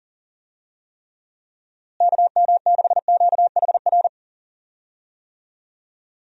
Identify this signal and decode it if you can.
Morse code, heard as a keyed tone, about 40 wpm, 700 Hz. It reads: KM6QHR
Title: USA Call Signs - 6 Characters - Repeated 2x - 40wpm